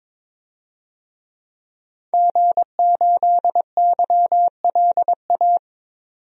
Morse code, heard as a keyed tone, about 22 wpm, 700 Hz. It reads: G8YLA
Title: DXCC-100 Call Signs - 5 Characters - 22wpm